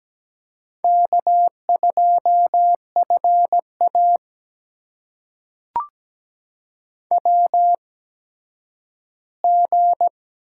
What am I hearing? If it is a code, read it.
Morse code, heard as a keyed tone, about 17 wpm, spaced as for 10 wpm, 700 Hz. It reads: K2FA WG
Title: USA Call Signs - 4 Characters - 8x Character Spacing - 17wpm